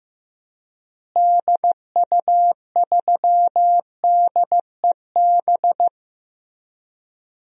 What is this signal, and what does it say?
Morse code, heard as a keyed tone, about 15 wpm, 700 Hz. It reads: DU3DEB
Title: DXCC-100 Call Signs - 6 Characters - Repeated 3x - 15wpm